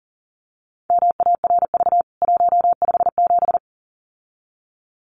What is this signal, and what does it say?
Morse code, heard as a keyed tone, about 40 wpm, 700 Hz. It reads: MARV 157
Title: CWT Exchange 40wpm